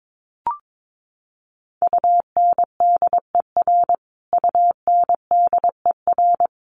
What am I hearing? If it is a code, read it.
Morse code, heard as a keyed tone, about 22 wpm, 700 Hz. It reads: UNDER UNDER
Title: Top 500 Words Repeated 22wpm